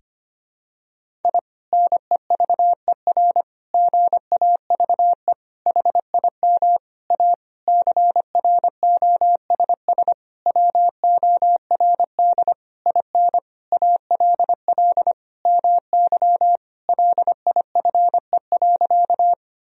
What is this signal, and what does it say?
Morse code, heard as a keyed tone, about 25 wpm, 700 Hz. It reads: I NEVER GAVE HIM A CROSS WORD IN ALL MY LIFE.
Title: Sentences from Top 1000 Words - Review Part 1 - 25wpm